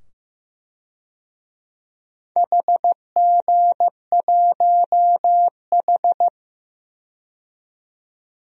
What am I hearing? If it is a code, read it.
Morse code, heard as a keyed tone, about 15 wpm, 700 Hz. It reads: HG1H
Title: DXCC-100 Call Signs - 4 Characters - 15wpm